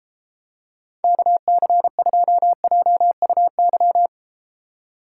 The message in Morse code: KC2JUY

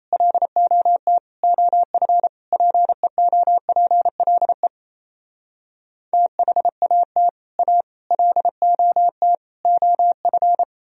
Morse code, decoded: LOT OF PEOPLE THAT A LOT OF

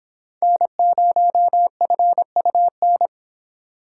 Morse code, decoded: N0FUN

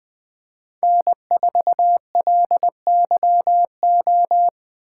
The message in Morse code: N4LYO

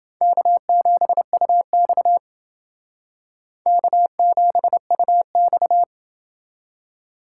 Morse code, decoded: K7UX K7UX

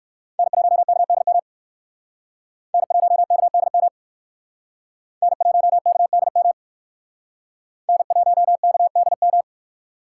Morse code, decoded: N1KDM N1KDM N1KDM N1KDM

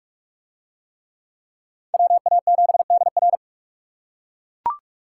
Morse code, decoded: WA7DR

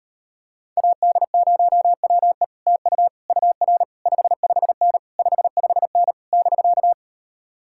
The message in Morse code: AD0WE TU UR 55N 55N <BK>

